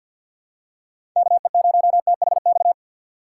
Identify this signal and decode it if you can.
Morse code, heard as a keyed tone, about 53 wpm, 700 Hz. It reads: KE0THX
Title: USA Call Signs - 2x Character Spacing - 50wpm